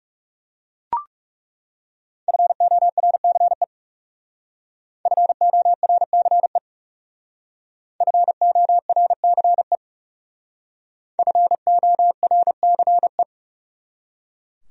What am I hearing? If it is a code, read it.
Morse code, heard as a keyed tone, about 38 wpm, 700 Hz. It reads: FORCE FORCE FORCE FORCE